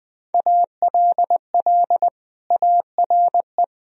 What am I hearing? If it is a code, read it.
Morse code, heard as a keyed tone, about 19 wpm, 700 Hz. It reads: ALL ARE
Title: Sentences from Top 1000 Words - Review Part 3 - 20wpm